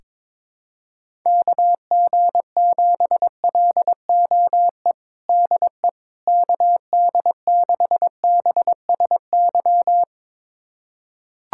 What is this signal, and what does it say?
Morse code, heard as a keyed tone, about 22 wpm, 700 Hz. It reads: KG7LOE DE KD6BSY
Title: USA Callsign DE Callsign - 6 Characters - 22wpm